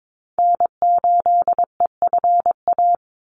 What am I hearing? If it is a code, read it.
Morse code, heard as a keyed tone, about 22 wpm, 700 Hz. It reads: N8EFA
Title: USA Call Signs 22wpm